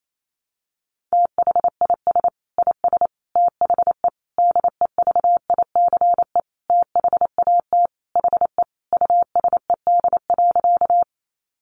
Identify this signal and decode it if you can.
Morse code, heard as a keyed tone, about 28 wpm, 700 Hz. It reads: THIS IS THE DEVICE THAT HE USED.
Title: Sentences from Top 1500 Words 28wpm